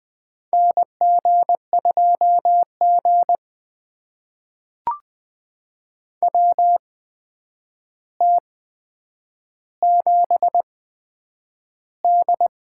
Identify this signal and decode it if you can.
Morse code, heard as a keyed tone, about 20 wpm, spaced as for 9 wpm, 700 Hz. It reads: NG2G WT7D
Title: USA Call Signs - 4 Characters - 8x Character Spacing - 20wpm